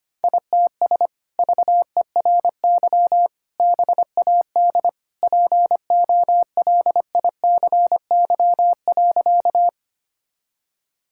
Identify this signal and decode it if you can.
Morse code, heard as a keyed tone, about 25 wpm, 700 Hz. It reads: ITS VERY BAD POLICY.
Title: Sentences from Top 1000 Words - Review Part 5 - 25wpm